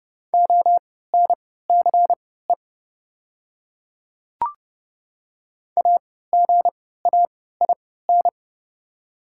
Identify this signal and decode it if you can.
Morse code, heard as a keyed tone, about 31 wpm, spaced as for 17 wpm, 700 Hz. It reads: ONCE AGAIN
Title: 3x Character Spacing - Top 300 Words - 30wpm